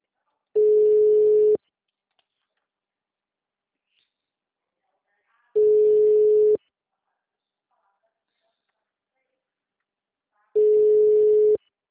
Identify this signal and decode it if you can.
Ringback tone (Europe, 425 Hz).
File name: out- -2006- - - .73733